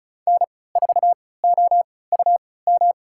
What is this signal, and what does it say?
Morse code, heard as a keyed tone, about 36 wpm, spaced as for 20 wpm, 700 Hz. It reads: N4OUM
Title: USA Call Signs - 5 Characters - 3x Character Spacing - 35wpm